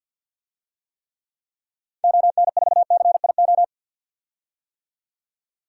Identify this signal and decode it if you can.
Morse code, heard as a keyed tone, about 49 wpm, 700 Hz. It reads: ON4CIO
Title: DXCC-100 Call Signs - 6 Characters - 50wpm